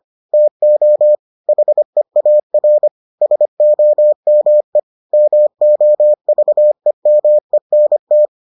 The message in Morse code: TO HEAR SOME MOVEMENT

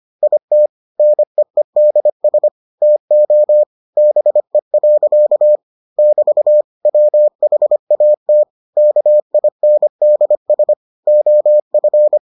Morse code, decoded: IT NEEDS TO BE. <BT> WHAT KINDS OF

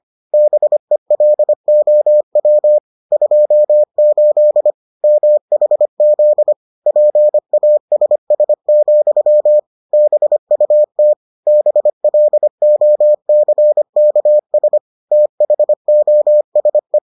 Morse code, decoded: BELOW 28 MHZ PASS, BUT BLOCKS THOSE

